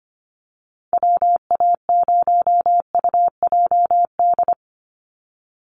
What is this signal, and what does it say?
Morse code, heard as a keyed tone, about 25 wpm, 700 Hz. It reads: WA0UJD
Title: USA Call Signs 25wpm